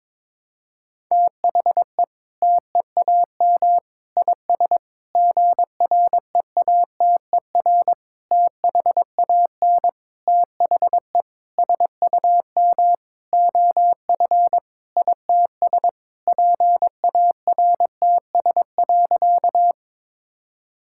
Morse code, heard as a keyed tone, about 22 wpm, 700 Hz. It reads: THE TEAM IS GREATER THAN THE SUM OF ITS PARTS.
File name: practice / 22wpm / Sentences with 8-Word Phrases - 22wpm